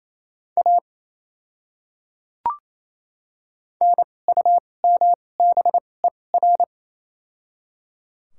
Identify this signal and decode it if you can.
Morse code, heard as a keyed tone, about 28 wpm, spaced as for 20 wpm, 700 Hz. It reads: A NUMBER
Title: 2x Character Spacing - Top 300 Words - 28wpm